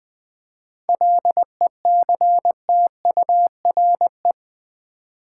Morse code, heard as a keyed tone, about 21 wpm, 700 Hz. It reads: LECTURE
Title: Seven Letter Words - 20wpm